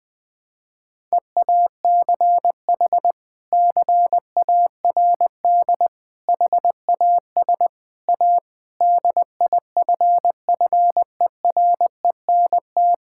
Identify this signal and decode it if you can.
Morse code, heard as a keyed tone, about 19 wpm, 700 Hz. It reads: EACH CARD HAS A DIFFERENT
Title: Sentences from Top 1000 Words - Review Part 6 - 20wpm